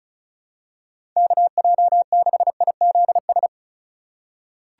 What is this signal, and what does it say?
Morse code, heard as a keyed tone, about 36 wpm, 700 Hz. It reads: KJ6IZS
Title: USA Call Signs - 6 Characters - 4x Character Spacing - 35wpm